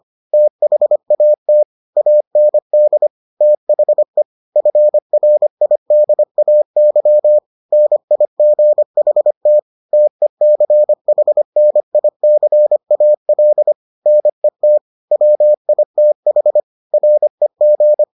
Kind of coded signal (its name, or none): Morse code